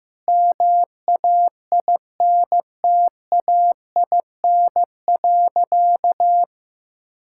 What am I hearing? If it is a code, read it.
Morse code, heard as a keyed tone, about 15 wpm, 700 Hz. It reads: MAINTAIN.
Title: Sentences from Top 1000 Words - Review Part 5 - 15wpm